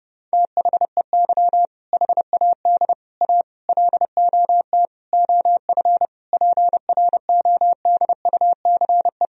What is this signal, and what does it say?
Morse code, heard as a keyed tone, about 30 wpm, 700 Hz. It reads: THEY HAD A LOT OF PRODUCE
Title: Sentences from Top 1000 Words - Review Part 3 - 30wpm